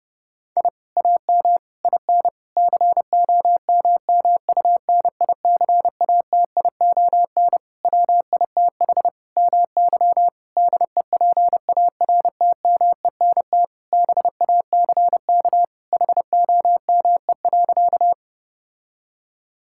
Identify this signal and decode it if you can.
Morse code, heard as a keyed tone, about 32 wpm, 700 Hz. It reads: I AM IN COMMUNICATION WITH MY DEPARTMENT BACK HOME.
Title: Sentences from Top 1000 Words - Review Part 6 - 30wpm